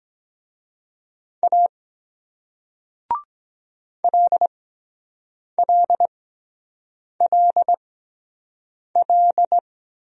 Morse code, heard as a keyed tone, about 23 wpm, spaced as for 7 wpm, 700 Hz. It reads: A LLLL